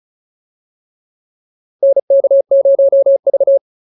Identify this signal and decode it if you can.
Morse code, heard as a keyed tone, about 34 wpm, 550 Hz. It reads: NK0V